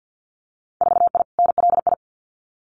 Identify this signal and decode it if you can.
Morse code, heard as a keyed tone, about 50 wpm, 700 Hz. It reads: 4I NLI
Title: Winter Field Day Exchanges Encore 50wpm